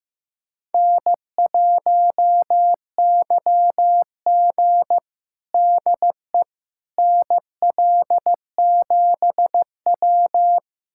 Morse code, heard as a keyed tone, about 15 wpm, 700 Hz. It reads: N1YG DE NL7W
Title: USA Callsign DE Callsign - 4 Characters - 15wpm